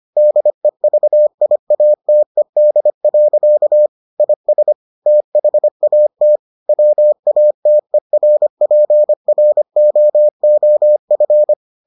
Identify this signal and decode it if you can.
Morse code, heard as a keyed tone, about 25 wpm, 600 Hz. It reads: DEVIATED. IS THAT WATERPROOF